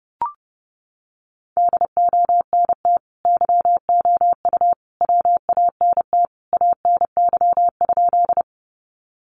Morse code, heard as a keyed tone, about 30 wpm, 700 Hz. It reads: DONT YOU WANT ANY?
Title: Sentences from Top 300 Words 30wpm